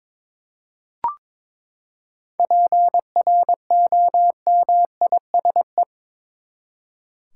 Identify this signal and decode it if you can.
Morse code, heard as a keyed tone, about 21 wpm, 700 Hz. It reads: PROMISE
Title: Seven Letter Words - 22wpm